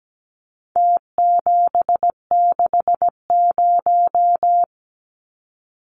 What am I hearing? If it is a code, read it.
Morse code, heard as a keyed tone, about 17 wpm, 700 Hz. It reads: T760
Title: Numbers - Contest Style - 4-Digit 17wpm